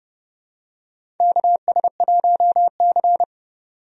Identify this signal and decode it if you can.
Morse code, heard as a keyed tone, about 32 wpm, 700 Hz. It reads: KS1C
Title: USA Call Signs - 4 Characters - 30wpm